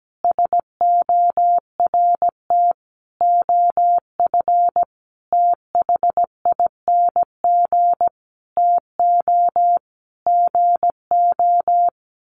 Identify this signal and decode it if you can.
Morse code, heard as a keyed tone, about 17 wpm, 700 Hz. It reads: SORT OF THING TO GO
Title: Sentences from Top 1300 Words 17wpm